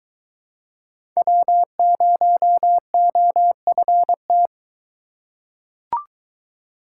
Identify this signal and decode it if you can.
Morse code, heard as a keyed tone, about 22 wpm, 700 Hz. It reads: W0OFT